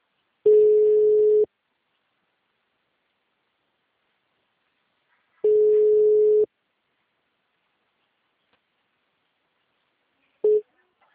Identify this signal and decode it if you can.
Ringback tone (Europe, 425 Hz).